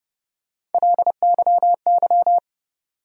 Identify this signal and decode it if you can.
Morse code, heard as a keyed tone, about 30 wpm, 700 Hz. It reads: LYY
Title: USA Call Sign Postfixes - 3 Letters 30wpm